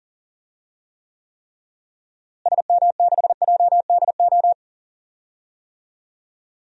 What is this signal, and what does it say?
Morse code, heard as a keyed tone, about 37 wpm, 700 Hz. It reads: SM6JDO
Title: DXCC-100 Call Signs - 6 Characters - Repeated 3x - 40wpm